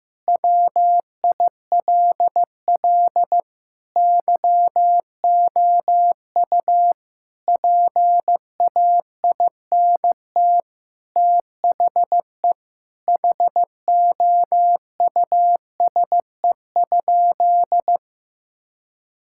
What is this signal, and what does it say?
Morse code, heard as a keyed tone, about 16 wpm, 700 Hz. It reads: WILL YOU PAINT THE HOUSE?
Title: Sentences from Top 1000 Words - Review Part 6 - 15wpm